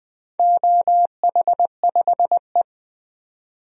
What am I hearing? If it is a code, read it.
Morse code, heard as a keyed tone, about 20 wpm, 700 Hz. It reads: OH5E
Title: DXCC-100 Call Signs - 4 Characters - 20wpm